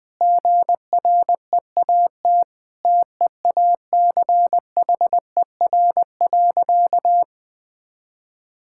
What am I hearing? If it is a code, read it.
Morse code, heard as a keyed tone, about 20 wpm, 700 Hz. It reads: GREAT TEACHER.
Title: Sentences from Top 1000 Words - Review Part 6 - 20wpm